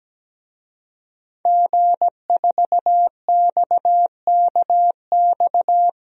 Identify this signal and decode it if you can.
Morse code, heard as a keyed tone, about 17 wpm, 700 Hz. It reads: G4XKX
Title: DXCC-100 Call Signs - 5 Characters - 17wpm